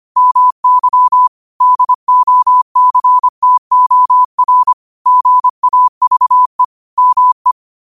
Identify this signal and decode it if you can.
Morse code, heard as a keyed tone, about 25 wpm, 1000 Hz. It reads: MY DOCTOR GAVE ME
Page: New quotes every day in morse code at 25 Words per minute.